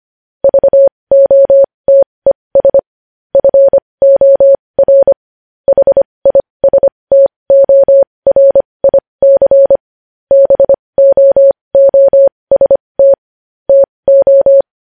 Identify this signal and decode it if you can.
Morse code, heard as a keyed tone, about 25 wpm, spaced as for 20 wpm, 550 Hz. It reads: VOTES FOR HISTORIC BOOST TO
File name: podcast / News headlines-1x-25x20